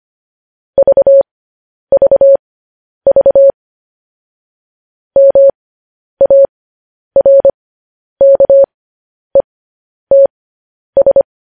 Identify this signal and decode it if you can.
Morse code, heard as a keyed tone, about 25 wpm, spaced as for 10 wpm, 550 Hz. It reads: VVV MARKETS
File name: podcast / News headlines-1x-25x10